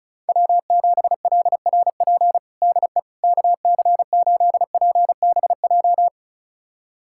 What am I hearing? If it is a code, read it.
Morse code, heard as a keyed tone, about 34 wpm, 700 Hz. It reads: W7LRP DE KC8PBJ
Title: USA Callsign DE Callsign 35wpm